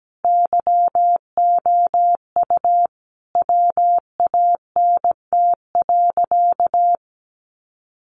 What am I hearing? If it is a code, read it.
Morse code, heard as a keyed tone, about 17 wpm, 700 Hz. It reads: YOU WANT.
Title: Sentences from Top 1100 Words 17wpm